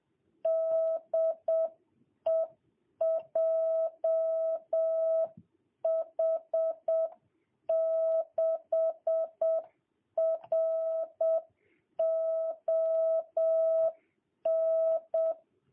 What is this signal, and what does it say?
Morse code, heard as a keyed tone, about 7 wpm, 660 Hz. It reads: DE JH6RON
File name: jh8cbh / 2022cw